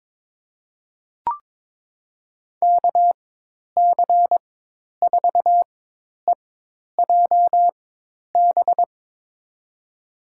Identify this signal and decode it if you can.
Morse code, heard as a keyed tone, about 23 wpm, spaced as for 10 wpm, 700 Hz. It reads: KC4EJB
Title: USA Call Signs - 6 Characters - 4x Character Spacing - 22wpm